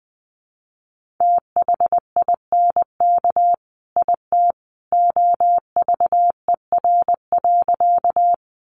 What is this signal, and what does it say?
Morse code, heard as a keyed tone, about 20 wpm, 700 Hz. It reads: THINK IT OVER.
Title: Sentences from Top 200 Words 20wpm